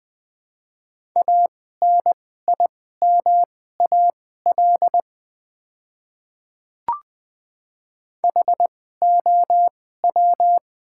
Morse code, heard as a keyed tone, about 20 wpm, spaced as for 14 wpm, 700 Hz. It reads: ANIMAL HOW
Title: 2x Character Spacing - Top 500 Words - 20wpm